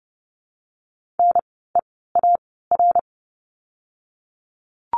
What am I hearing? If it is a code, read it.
Morse code, heard as a keyed tone, about 30 wpm, 700 Hz. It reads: N E A R
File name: Lesson 16f - Letter c - Words 30wpm